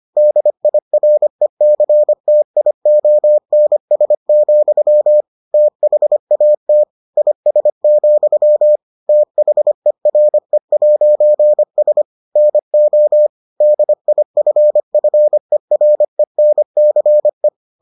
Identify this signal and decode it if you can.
Morse code, heard as a keyed tone, about 25 wpm, 600 Hz. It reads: DIRECTIONS, THAT IS, THERE'S NO DIFFERENCE